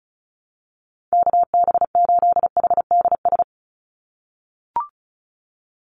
Morse code, heard as a keyed tone, about 34 wpm, 700 Hz. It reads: KB8HDS
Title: USA Call Signs 35wpm